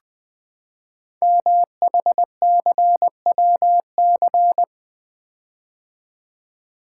Morse code, heard as a keyed tone, about 20 wpm, 700 Hz. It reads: MHCWC
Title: Code Groups - 5 Letter:Numbers - 20wpm